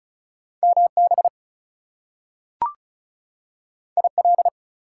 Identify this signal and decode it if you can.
Morse code, heard as a keyed tone, about 36 wpm, 700 Hz. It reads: MB IL
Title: US States and Canadian Provinces 35wpm